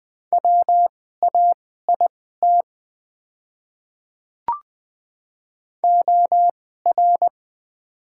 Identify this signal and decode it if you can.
Morse code, heard as a keyed tone, about 20 wpm, spaced as for 14 wpm, 700 Hz. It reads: WAIT OR
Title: 2x Character Spacing - Top 500 Words - 20wpm